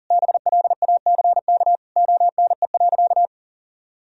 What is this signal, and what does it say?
Morse code, heard as a keyed tone, about 40 wpm, 700 Hz. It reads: BLACK ONE.
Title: Sentences from Top 1000 Words - Review Part 2 - 40wpm